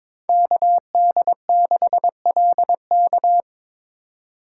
Morse code, heard as a keyed tone, about 22 wpm, 700 Hz. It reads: KD6LK
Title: USA Call Signs - 2x Character Spacing - 22wpm